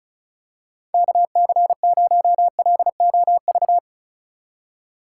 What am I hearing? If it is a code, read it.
Morse code, heard as a keyed tone, about 36 wpm, 700 Hz. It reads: KC0LOV
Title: USA Call Signs - 4x Character Spacing - 35wpm